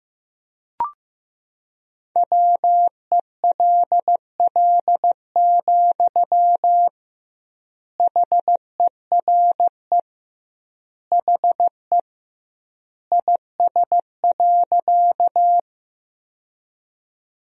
Morse code, heard as a keyed tone, about 16 wpm, 700 Hz. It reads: WELL, HERE HE IS.
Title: 2x Word-Spacing - Sentences from Top 200 Words - 15wpm